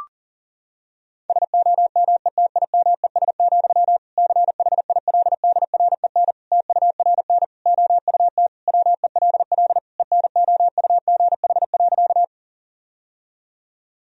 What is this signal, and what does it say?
Morse code, heard as a keyed tone, about 40 wpm, 700 Hz. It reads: SOMETIMES, CHILDREN TURN OUT WELL ENOUGH.
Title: Sentences from Top 1000 Words - Review Part 3 - 40wpm